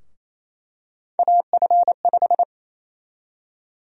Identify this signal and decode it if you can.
Morse code, heard as a keyed tone, about 28 wpm, 700 Hz. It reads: AF5
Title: USA Call Sign Prefixes - Repeated 2x - 28wpm